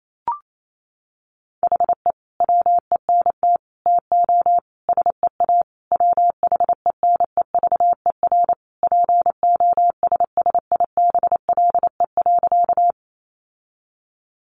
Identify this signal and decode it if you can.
Morse code, heard as a keyed tone, about 28 wpm, 700 Hz. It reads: HE WENT TO SEA WHENEVER POSSIBLE.
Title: Sentences from Top 1200 Words 28wpm